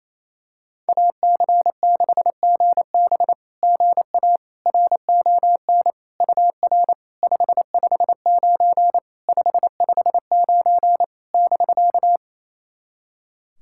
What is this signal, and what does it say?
Morse code, heard as a keyed tone, about 28 wpm, 700 Hz. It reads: AC6GB GA RON UR 559 559 <BK>
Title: SOTA QSO Practice 28wpm